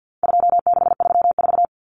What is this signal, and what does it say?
Morse code, heard as a keyed tone, about 50 wpm, 700 Hz. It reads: 2634
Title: Numbers - 4-Digit 50wpm